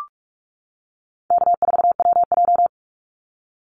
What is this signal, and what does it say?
Morse code, heard as a keyed tone, about 46 wpm, 700 Hz. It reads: K4WJ
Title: Call Signs - 45wpm